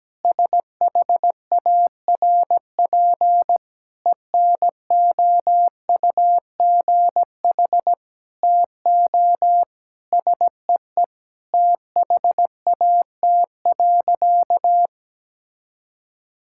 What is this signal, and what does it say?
Morse code, heard as a keyed tone, about 17 wpm, 700 Hz. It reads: SHARP ENOUGH TO SEE THAT.
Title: Sentences from Top 1600 Words 17wpm